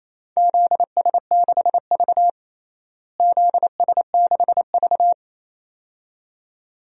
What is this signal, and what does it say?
Morse code, heard as a keyed tone, about 28 wpm, 700 Hz. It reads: ZS6V ZS6V
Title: DXCC-100 Call Signs - 4 Characters - Repeated 2x - 28wpm